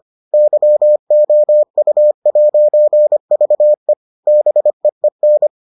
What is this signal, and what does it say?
Morse code, heard as a keyed tone, about 25 wpm, 600 Hz. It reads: YOU'VE BEEN